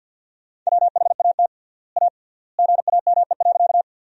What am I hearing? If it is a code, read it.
Morse code, heard as a keyed tone, about 47 wpm, 700 Hz. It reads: WHAT A GAME.
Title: 3x Word-Spacing - Sentences from Top 500 Words - 50wpm